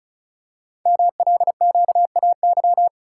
Morse code, heard as a keyed tone, about 36 wpm, 690 Hz. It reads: MLQAY
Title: Code Groups - 5 Letters - 35wpm